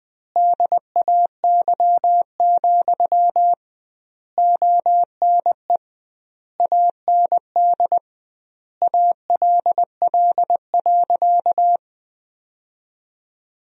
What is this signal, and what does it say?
Morse code, heard as a keyed tone, about 20 wpm, 700 Hz. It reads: DAY, ONE AND ALL.
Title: 2x Word-Spacing - Sentences from Top 200 Words - 20wpm